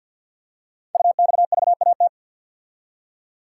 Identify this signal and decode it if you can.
Morse code, heard as a keyed tone, about 51 wpm, 700 Hz. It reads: UXVAT
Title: Code Groups - 5 Letters - 50wpm